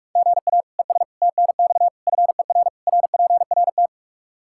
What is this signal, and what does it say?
Morse code, heard as a keyed tone, about 46 wpm, 700 Hz. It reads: GA ES TNX FER RPRT
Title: Practice QSO - Exchange 1 - Beginning - 45wpm